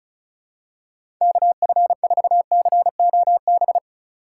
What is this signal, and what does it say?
Morse code, heard as a keyed tone, about 36 wpm, 700 Hz. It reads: KF4COB
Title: USA Call Signs - 6 Characters - 2x Character Spacing - 35wpm